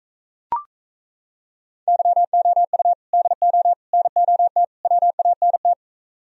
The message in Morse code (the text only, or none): YOU DO NOT WANT